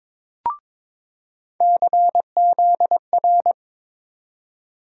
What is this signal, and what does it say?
Morse code, heard as a keyed tone, about 22 wpm, 700 Hz. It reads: CZR
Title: USA Call Sign Postfixes - 3 Letters 22wpm